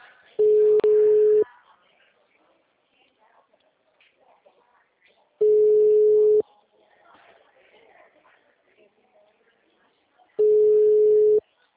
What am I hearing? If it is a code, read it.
Ringback tone (Europe, 425 Hz), with speech.